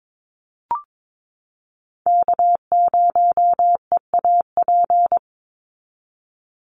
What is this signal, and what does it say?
Morse code, heard as a keyed tone, about 22 wpm, 700 Hz. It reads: K0EAP
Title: NA SOTA Chaser Callsigns 22wpm